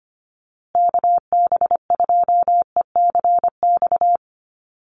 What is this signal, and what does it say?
Morse code, heard as a keyed tone, about 25 wpm, 700 Hz. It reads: KB2ECX
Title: USA Call Signs Repeated 3x 25wpm